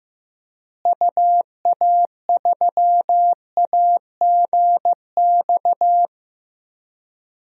Morse code, heard as a keyed tone, about 15 wpm, 700 Hz. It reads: UA3AGX
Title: DXCC-100 Call Signs - 6 Characters - Repeated 3x - 15wpm